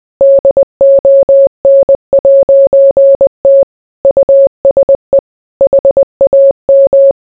Morse code, heard as a keyed tone, about 20 wpm, 550 Hz. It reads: DON'T USE HAM